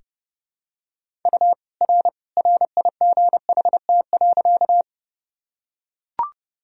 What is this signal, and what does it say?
Morse code, heard as a keyed tone, about 32 wpm, 700 Hz. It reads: U R RIGHT.
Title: Sentences from Top 200 Words and Abbreviations 30wpm